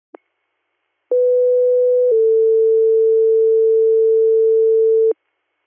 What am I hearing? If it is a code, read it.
Two-tone page, heard as a one-second tone, then a three-second tone, about 500 then 450 Hz. It is heard from 1.1 to 5.1 s.